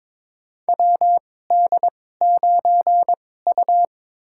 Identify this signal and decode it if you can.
Morse code, heard as a keyed tone, about 22 wpm, spaced as for 16 wpm, 700 Hz. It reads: WD9U
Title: USA Call Signs - 2x Character Spacing - 22wpm